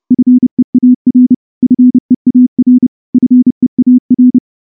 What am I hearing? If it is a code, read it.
Morse code, heard as a keyed tone, about 28 wpm, 250 Hz. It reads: FEAR FEAR FEAR
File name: morse-training-30WPM